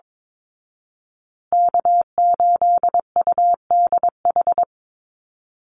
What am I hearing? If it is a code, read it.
Morse code, heard as a keyed tone, about 22 wpm, 690 Hz. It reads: K8UDH
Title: Call Signs Repeated 3x - 22wpm